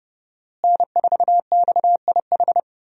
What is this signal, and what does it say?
Morse code, heard as a keyed tone, about 30 wpm, 700 Hz. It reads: N4XIH
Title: USA Call Signs - 5 Characters - Repeated 2x - 30wpm